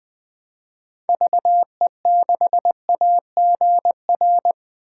Morse code, heard as a keyed tone, about 20 wpm, 700 Hz. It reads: VE6AGR
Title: 2023 SOTA Chaser Call Signs - 20wpm